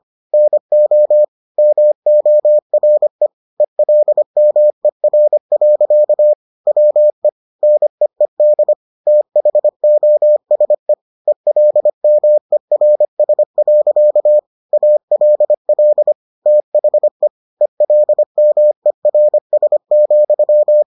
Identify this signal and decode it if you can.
Morse code, heard as a keyed tone, about 25 wpm, 600 Hz. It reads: NO MORE ELMER. WE NEED THOSE ELMERS. ALL THE ELMERS,